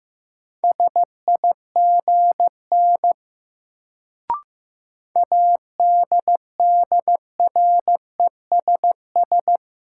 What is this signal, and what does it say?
Morse code, heard as a keyed tone, about 15 wpm, 700 Hz. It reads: SIGN ADDRESS
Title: Sentences from Top 1000 Words - New Word Drill - 15wpm